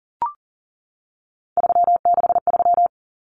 Morse code, heard as a keyed tone, about 40 wpm, 700 Hz. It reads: 363
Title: Common Cut Numbers - 3-Digit 40wpm